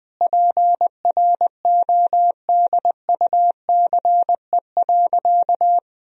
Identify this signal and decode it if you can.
Morse code, heard as a keyed tone, about 20 wpm, 700 Hz. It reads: PRODUCE.
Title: Sentences from Top 1000 Words - Review Part 3 - 20wpm